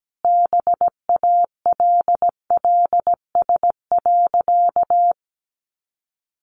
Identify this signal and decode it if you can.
Morse code, heard as a keyed tone, about 17 wpm, 700 Hz. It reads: BALLS.
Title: Easy Phrases 17wpm